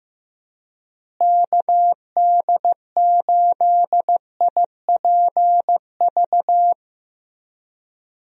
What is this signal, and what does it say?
Morse code, heard as a keyed tone, about 16 wpm, 700 Hz. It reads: KD8IPV
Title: USA Call Signs - 6 Characters - Repeated 2x - 15wpm